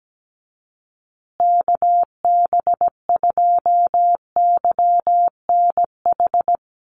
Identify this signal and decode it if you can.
Morse code, heard as a keyed tone, about 17 wpm, 700 Hz. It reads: KB2YNH
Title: USA Call Signs Repeated 3x 17wpm